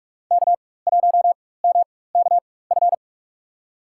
Morse code, heard as a keyed tone, about 46 wpm, spaced as for 21 wpm, 700 Hz. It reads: K1MKF
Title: USA Call Signs - 4x Character Spacing - 45wpm